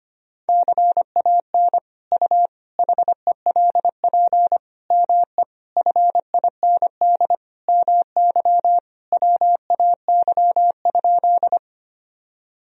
Morse code, heard as a keyed tone, about 25 wpm, 700 Hz. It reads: CAN U HELP ME FIND MY WAY?
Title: Sentences from Top 200 Words and Abbreviations 25wpm